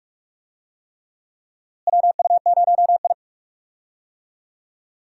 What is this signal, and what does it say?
Morse code, heard as a keyed tone, about 46 wpm, 700 Hz. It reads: WU0I
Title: USA Call Signs - 4 Characters - Repeated 3x - 45wpm